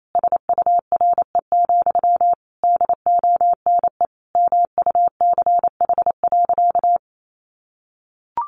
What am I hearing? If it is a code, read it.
Morse code, heard as a keyed tone, about 28 wpm, 700 Hz. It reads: SURE, DONE MUCH.
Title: Sentences from Top 500 Words 28wpm